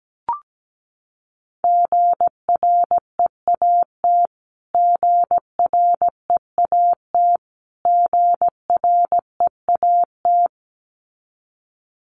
Morse code, heard as a keyed tone, about 17 wpm, 700 Hz. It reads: GREAT GREAT GREAT
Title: Top 500 Words Repeated 17wpm